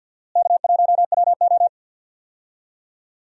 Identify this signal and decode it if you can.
Morse code, heard as a keyed tone, about 53 wpm, 700 Hz. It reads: K1WO
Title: USA Call Signs - 4 Characters - Repeated 2x - 50wpm